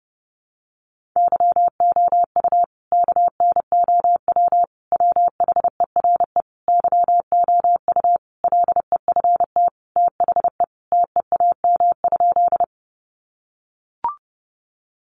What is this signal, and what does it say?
Morse code, heard as a keyed tone, about 30 wpm, 700 Hz. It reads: YOU KNOW WHERE YOU LEFT THE TEAM?
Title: Sentences from Top 1500 Words 30wpm